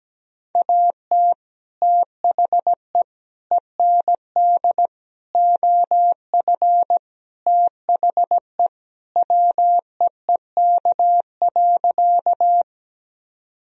Morse code, heard as a keyed tone, about 17 wpm, 700 Hz. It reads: AT THE END OF THE WEEK.
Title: Sentences from Top 1000 Words - Review Part 2 - 17wpm